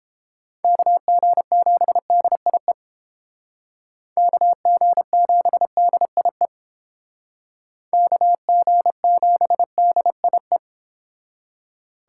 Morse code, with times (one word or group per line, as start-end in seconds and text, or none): 0.6-2.7 KG7DIE
4.2-6.5 KG7DIE
7.9-10.6 KG7DIE